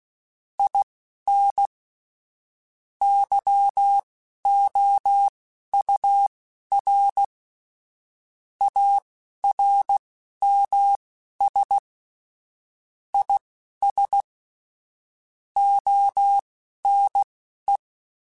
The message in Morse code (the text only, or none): IN YOUR ARMS IS ONE